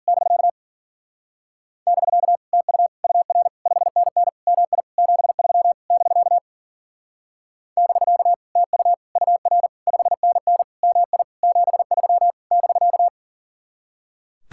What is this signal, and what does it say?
Morse code, heard as a keyed tone, about 49 wpm, 700 Hz. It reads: <BK> <BK> TU UR 5NN MI 73 <BK> <BK> TU UR 5NN MI 73 <BK>
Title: Speed-Racing Common QSO Phrases Encore - 40wpm